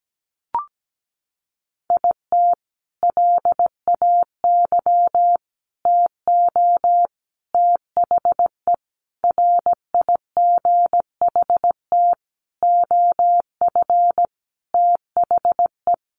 Morse code, heard as a keyed tone, about 17 wpm, 700 Hz. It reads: IT LAY TO THE RIGHT OF THE
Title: Sentences from Top 1400 Words 17wpm